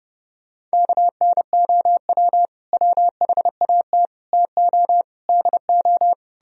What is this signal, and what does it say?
Morse code, heard as a keyed tone, about 40 wpm, 700 Hz. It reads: KNOW WHAT TO DO